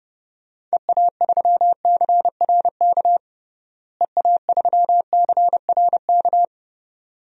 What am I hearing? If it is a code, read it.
Morse code, heard as a keyed tone, about 30 wpm, 700 Hz. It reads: EA3CRK EA3CRK
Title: DXCC-100 Call Signs - 6 Characters - Repeated 2x - 30wpm